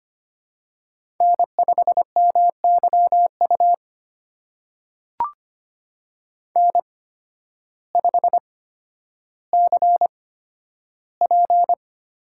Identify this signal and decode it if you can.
Morse code, heard as a keyed tone, about 25 wpm, spaced as for 12 wpm, 700 Hz. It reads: N5MYU N5CP
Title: USA Call Signs - 5 Characters - 8x Character Spacing - 25wpm